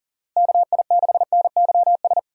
Morse code, heard as a keyed tone, about 40 wpm, 700 Hz. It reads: KI6NYS
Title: USA Call Signs - 6 Characters - Repeated 3x - 40wpm